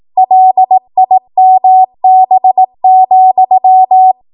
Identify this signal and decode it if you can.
Morse code, heard as a keyed tone, about 18 wpm, 750 Hz. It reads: LIMB,
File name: Morse / 18WPM